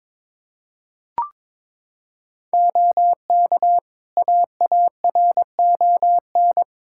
Morse code, heard as a keyed tone, about 21 wpm, 700 Hz. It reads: OK AARON
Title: Practice QSO - Exchange 2 - Info - 22wpm